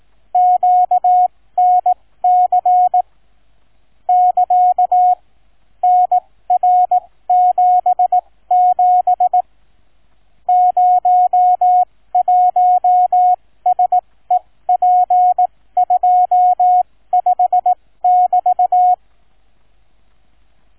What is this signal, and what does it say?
Morse code, heard as a keyed tone, about 18 wpm, spaced as for 15 wpm, 730 Hz. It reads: QNC <KA> NR77 01SEP25 <BT>
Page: Shortened example of QNC Sound file